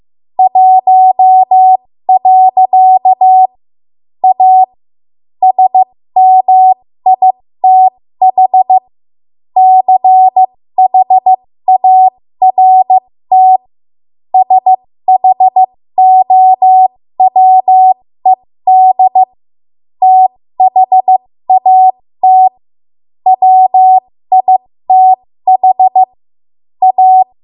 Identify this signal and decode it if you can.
Morse code, heard as a keyed tone, about 15 wpm, spaced as for 13 wpm, 750 Hz. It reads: 1. A SMITH CHART SHOWED THAT WITH A
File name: Morse / 13WPM